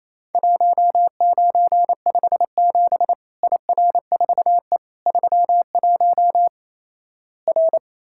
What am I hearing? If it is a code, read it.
Morse code, heard as a keyed tone, about 28 wpm, 700 Hz. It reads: 1957 IR4E 31 R
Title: Worked All Europe DX Contest - QTC Exchange - 28wpm